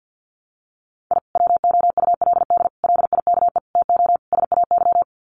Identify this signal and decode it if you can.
Morse code, heard as a keyed tone, about 49 wpm, 700 Hz. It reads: I WOULD LIKE TO SAY
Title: Sets of 5 Words Encore2 50wpm